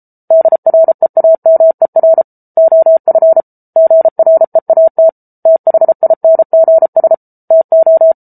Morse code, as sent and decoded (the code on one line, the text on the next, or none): -.. .-. .- -- . .-. / --- ..-. / --. .-. .- - / - .... .. -. --. ... / - ---
DREAMER OF GREAT THINGS TO